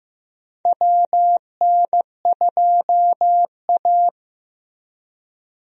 Morse code, heard as a keyed tone, about 15 wpm, 700 Hz. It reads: WN2A
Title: USA Call Signs - 4 Characters - Repeated 2x - 15wpm